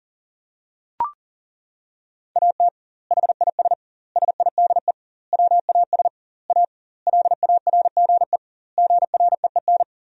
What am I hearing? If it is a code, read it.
Morse code, heard as a keyed tone, about 40 wpm, 700 Hz. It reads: AT HIS SIDE WAS A LARGE GREEN
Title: 2x Word-Spacing - Sentences from Top 500 Words - 40wpm